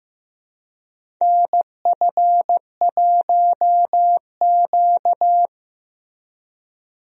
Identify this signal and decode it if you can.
Morse code, heard as a keyed tone, about 15 wpm, 700 Hz. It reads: NF1Q
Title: USA Call Signs - 4 Characters - Repeated 3x - 15wpm